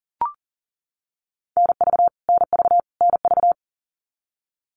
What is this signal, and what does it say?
Morse code, heard as a keyed tone, about 40 wpm, 700 Hz. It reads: NV NV NV
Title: US States Repeated 3x 40wpm